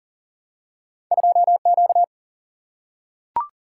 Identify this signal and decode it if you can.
Morse code, heard as a keyed tone, about 40 wpm, 700 Hz. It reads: 2 Q
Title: Single-Letter Single-Number 40wpm